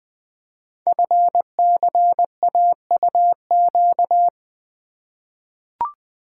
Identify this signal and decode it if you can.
Morse code, heard as a keyed tone, about 20 wpm, 700 Hz. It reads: FCAUQ
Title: Code Groups - 5 Letters - 20wpm